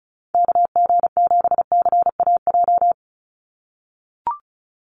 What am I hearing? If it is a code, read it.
Morse code, heard as a keyed tone, about 36 wpm, 700 Hz. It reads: KG7CAJ
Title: USA Call Signs 35wpm